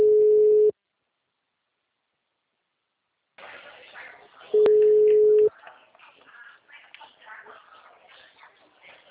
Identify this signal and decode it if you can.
Ringback tone (Europe, 425 Hz).